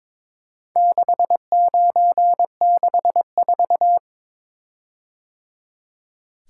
Morse code, heard as a keyed tone, about 23 wpm, 700 Hz. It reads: 6964
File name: practice / 22wpm / Four Letter-Number Combinations - 22wpm